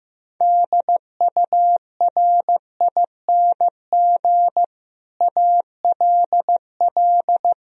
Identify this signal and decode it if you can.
Morse code, heard as a keyed tone, about 15 wpm, 700 Hz. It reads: DURING ALL
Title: Sentences from Top 1000 Words - Review Part 3 - 15wpm